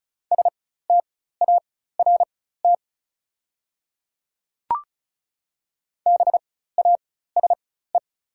Morse code, heard as a keyed tone, about 34 wpm, spaced as for 16 wpm, 690 Hz. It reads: START BASE
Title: 4x Character Spacing - Top 300 Words - 35wpm